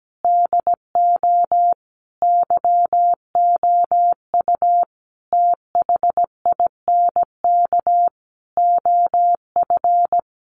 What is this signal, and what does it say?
Morse code, heard as a keyed tone, about 17 wpm, 700 Hz. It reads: DO YOU THINK OF
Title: Sets of 5 Words 17wpm